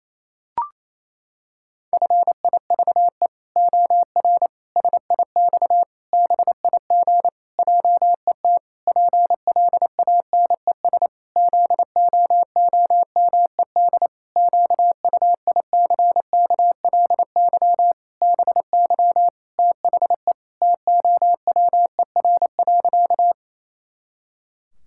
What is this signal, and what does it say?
Morse code, heard as a keyed tone, about 28 wpm, 700 Hz. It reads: FIVE OR SIX BIG JET PLANES ZOOMED QUICKLY BY THE TOWER.
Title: Pangram Challenge v2 - 28wpm